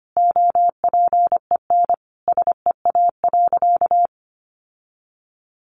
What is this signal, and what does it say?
Morse code, heard as a keyed tone, about 25 wpm, 700 Hz. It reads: OPEN SEA.
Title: Sentences from Top 1500 Words 25wpm